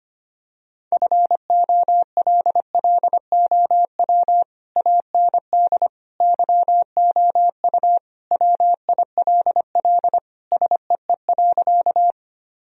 Morse code, heard as a keyed tone, about 25 wpm, 700 Hz. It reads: FOLLOW AND YOU WILL SEE.
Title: Sentences from Top 1000 Words - Review Part 1 - 25wpm